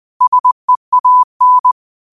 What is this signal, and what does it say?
Morse code, heard as a keyed tone, about 20 wpm, 1000 Hz. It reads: SEAN
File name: qotdfiles / 20wpm